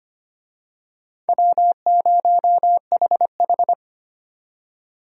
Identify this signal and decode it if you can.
Morse code, heard as a keyed tone, about 25 wpm, 700 Hz. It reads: W0HH
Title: USA Call Signs - 4 Characters - 25wpm